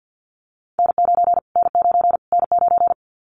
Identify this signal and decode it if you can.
Morse code, heard as a keyed tone, about 54 wpm, 700 Hz. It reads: N9 N9 N9
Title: Callsign Prefixes for Top 25 DXCC - Repeated 3x - 50wpm